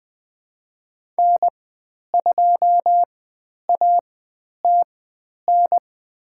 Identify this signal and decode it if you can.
Morse code, heard as a keyed tone, about 20 wpm, spaced as for 10 wpm, 700 Hz. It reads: N2ATN
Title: 109 Lesson 18g - Letter m - Call Signs 20wpm